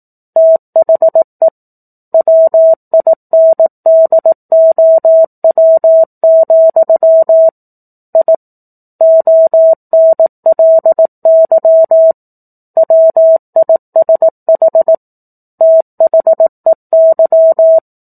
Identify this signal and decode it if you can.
Morse code, heard as a keyed tone, about 18 wpm, 650 Hz. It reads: THE WINDOW, I ONLY WISH THEY
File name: 04-Chapter-04 18 WPM